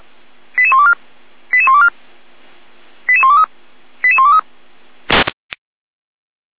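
Five-tone selective call (ZVEI), heard as quick runs of five tones, tones of about 70 ms. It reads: 88124 88124 88123 88123
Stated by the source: ZVEI five-tone selective call 88191 received by SDR.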